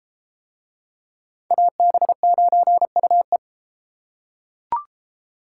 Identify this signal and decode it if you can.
Morse code, heard as a keyed tone, about 29 wpm, 700 Hz. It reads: AB9UE